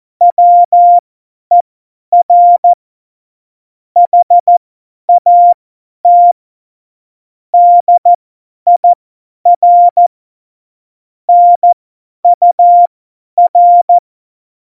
Morse code, heard as a keyed tone, about 14 wpm, spaced as for 10 wpm, 700 Hz. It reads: WER HAT DIR NUR
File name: Morsekurs / Witze / Witz 10 April 2018, 50 F70